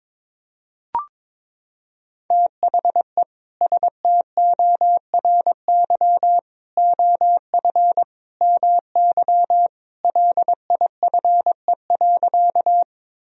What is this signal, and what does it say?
Morse code, heard as a keyed tone, about 22 wpm, 700 Hz. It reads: THE STORY OF MY LIFE.
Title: Sentences from Top 1000 Words - Review Part 1 - 22wpm